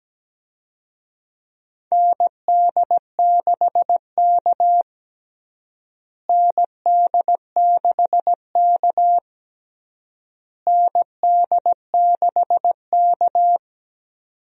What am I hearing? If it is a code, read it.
Morse code, heard as a keyed tone, about 17 wpm, 700 Hz. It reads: ND6K ND6K ND6K
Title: USA Call Signs - 4 Characters - Repeated 3x - 17wpm